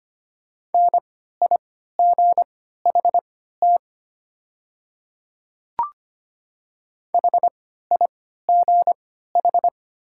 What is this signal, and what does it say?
Morse code, heard as a keyed tone, about 25 wpm, spaced as for 14 wpm, 690 Hz. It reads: NIGHT HIGH
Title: 3x Character Spacing - Top 300 Words - 25wpm